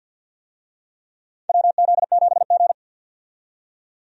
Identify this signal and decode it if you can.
Morse code, heard as a keyed tone, about 52 wpm, 700 Hz. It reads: WZ7G
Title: USA Call Signs - 4 Characters - 50wpm